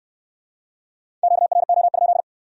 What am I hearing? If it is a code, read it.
Morse code, heard as a keyed tone, about 63 wpm, 700 Hz. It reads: QRO?